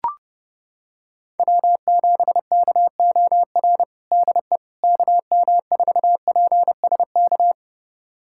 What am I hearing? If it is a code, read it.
Morse code, heard as a keyed tone, about 30 wpm, 700 Hz. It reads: W7KOR DE KM4PSK
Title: USA Callsign DE Callsign 30wpm